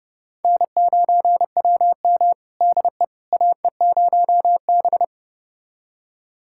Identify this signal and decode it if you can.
Morse code, heard as a keyed tone, about 30 wpm, 700 Hz. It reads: N9WM DE AE0B
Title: USA Callsign DE Callsign - 4 Characters - 30wpm